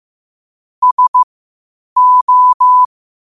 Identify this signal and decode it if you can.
Morse code, heard as a keyed tone, about 15 wpm, spaced as for 8 wpm, 1000 Hz. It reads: SO
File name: qotdfiles / 8wpm